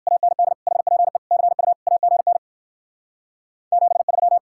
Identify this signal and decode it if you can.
Morse code, heard as a keyed tone, about 60 wpm, 700 Hz. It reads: AND HPE CU AGN 73